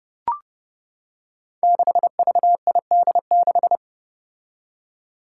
Morse code, heard as a keyed tone, about 30 wpm, 700 Hz. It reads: 6VID6
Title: Five Challenging Letter-Number Combinations 30wpm